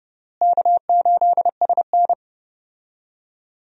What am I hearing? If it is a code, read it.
Morse code, heard as a keyed tone, about 32 wpm, 700 Hz. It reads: K8SN
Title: USA Call Signs - 4 Characters - Repeated 2x - 30wpm